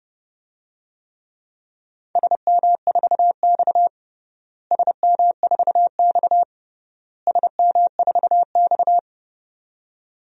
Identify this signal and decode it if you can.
Morse code, heard as a keyed tone, about 32 wpm, 700 Hz. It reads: SM4X SM4X SM4X
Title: DXCC-100 Call Signs - 4 Characters - Repeated 3x - 30wpm